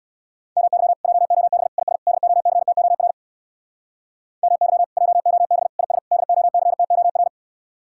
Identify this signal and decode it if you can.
Morse code, heard as a keyed tone, about 60 wpm, 700 Hz. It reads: MY JOB IS DOCTOR MY JOB IS DOCTOR
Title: Speed-Racing Common QSO Phrases - 50wpm